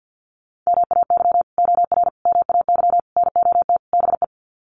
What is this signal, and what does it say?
Morse code, heard as a keyed tone, about 50 wpm, 700 Hz. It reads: MAY OR MAY NOT BE
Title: Sets of 5 Words 50wpm